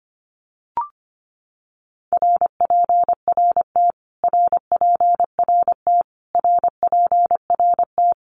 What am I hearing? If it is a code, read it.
Morse code, heard as a keyed tone, about 25 wpm, 700 Hz. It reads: RPRT RPRT RPRT
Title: Abbreviations Repeated 3x - 25wpm